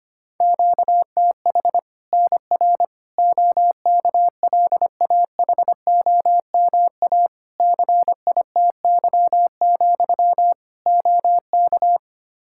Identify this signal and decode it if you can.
Morse code, heard as a keyed tone, about 25 wpm, 700 Hz. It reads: QTH NR OKLAHOMA CITY, OK
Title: Practice QSO - Exchange 1 - Beginning - 25wpm